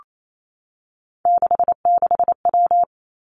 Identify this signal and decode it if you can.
Morse code, heard as a keyed tone, about 28 wpm, 700 Hz. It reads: 66W
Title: Three Challenging Letter-Number Combinations 28wpm